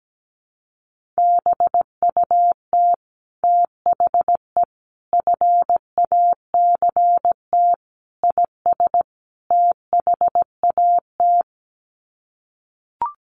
Morse code, heard as a keyed tone, about 17 wpm, 700 Hz. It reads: BUT THE FACT IS THAT
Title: Sets of 5 Words Encore2 17wpm